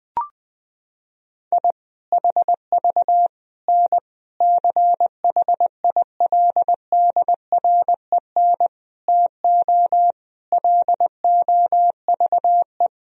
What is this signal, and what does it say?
Morse code, heard as a keyed tone, about 20 wpm, 700 Hz. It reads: I HV N CHILDREN TO LOVE
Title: Sentences from Top 400 Words and Abbreviations 20wpm